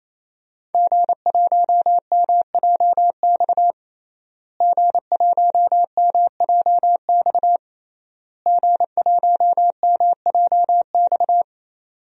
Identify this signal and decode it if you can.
Morse code, heard as a keyed tone, about 28 wpm, 700 Hz. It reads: G1MJX G1MJX G1MJX
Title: DXCC-100 Call Signs - 5 Characters - Repeated 3x - 28wpm